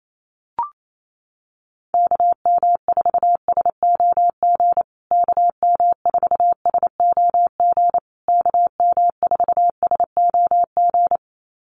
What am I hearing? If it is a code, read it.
Morse code, heard as a keyed tone, about 28 wpm, 700 Hz. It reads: KM4SOG KM4SOG KM4SOG
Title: USA Call Signs Repeated 3x 28wpm